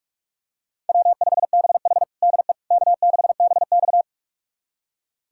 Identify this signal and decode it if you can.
Morse code, heard as a keyed tone, about 46 wpm, 700 Hz. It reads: W5BH DE K6BX
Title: USA Callsign DE Callsign - 4 Characters - 45wpm